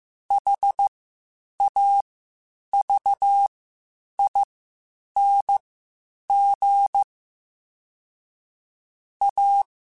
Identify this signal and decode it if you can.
Morse code, heard as a keyed tone, about 15 wpm, spaced as for 8 wpm, 800 Hz. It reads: HAVING A
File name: Quotes Spaced / Quotes Spaced-mp3s / ChildAsleep sp @13